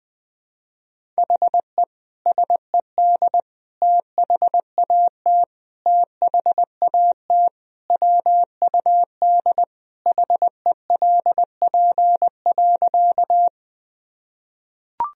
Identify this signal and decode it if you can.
Morse code, heard as a keyed tone, about 20 wpm, 700 Hz. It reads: HE SED THAT THAT WUD HELP.
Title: Sentences from Top 200 Words and Abbreviations 20wpm